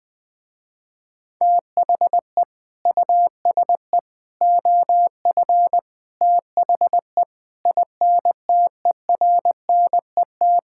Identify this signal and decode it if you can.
Morse code, heard as a keyed tone, about 20 wpm, 700 Hz. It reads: THE USE OF THE INTERNET
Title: Sentences with 4-Word Phrases - Spoken Beginning - 20wpm